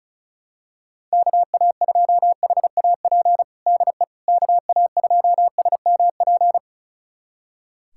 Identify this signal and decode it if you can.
Morse code, heard as a keyed tone, about 34 wpm, 700 Hz. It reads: KA2HAP DE KA2SMP
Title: USA Callsign DE Callsign - 6 Characters - 35wpm